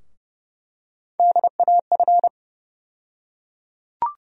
Morse code, heard as a keyed tone, about 30 wpm, 700 Hz. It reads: DAF
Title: USA Call Sign Postfixes - 3 Letters 30wpm